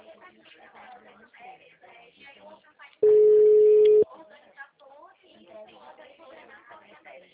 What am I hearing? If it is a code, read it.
Ringback tone (Europe, 425 Hz), with speech.
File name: out- -2013- - - .27178